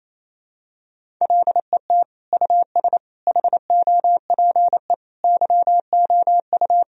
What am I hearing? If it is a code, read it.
Morse code, heard as a keyed tone, about 28 wpm, 700 Hz. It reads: LET US HOPE YOU
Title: Sentences from Top 1000 Words - Review Part 1 - 28wpm